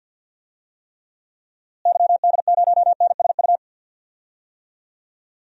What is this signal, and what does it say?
Morse code, heard as a keyed tone, about 50 wpm, 700 Hz. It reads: YD0NSU
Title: DXCC-100 Call Signs - 6 Characters - Repeated 3x - 50wpm